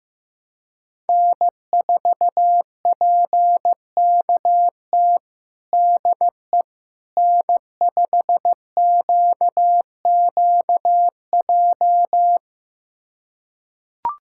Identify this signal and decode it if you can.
Morse code, heard as a keyed tone, about 15 wpm, 700 Hz. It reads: N4PKT DE N5QQJ
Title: USA Callsign DE Callsign - 5 Characters - 15wpm